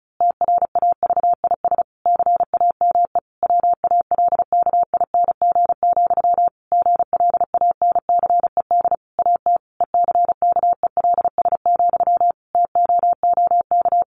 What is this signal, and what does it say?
Morse code, heard as a keyed tone, about 34 wpm, 700 Hz. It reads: TRAVIS CAME WALKING, GLANCED AT ECKELS, TOOK
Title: A Sound Of Thunder 35wpm